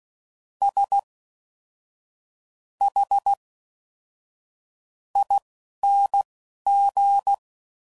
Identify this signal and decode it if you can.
Morse code, heard as a keyed tone, about 16 wpm, spaced as for 6 wpm, 800 Hz. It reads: SHING